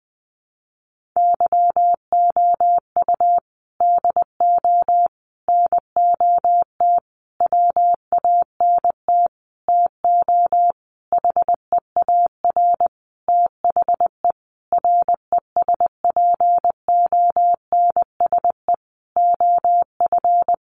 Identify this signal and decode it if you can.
Morse code, heard as a keyed tone, about 20 wpm, 700 Hz. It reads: YOU DO NOT WANT TO HEAR THE RESPONSE OF
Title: Sentences from Top 1400 Words 20wpm